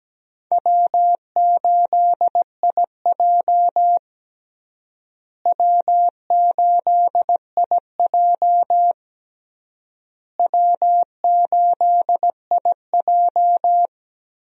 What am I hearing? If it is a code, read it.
Morse code, heard as a keyed tone, about 17 wpm, 700 Hz. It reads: W8IJ W8IJ W8IJ
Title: USA Call Signs - 4 Characters - Repeated 3x - 17wpm